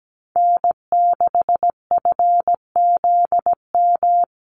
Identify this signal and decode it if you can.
Morse code, heard as a keyed tone, about 17 wpm, 700 Hz. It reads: N6FZM
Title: USA Call Signs Repeated 2x 17wpm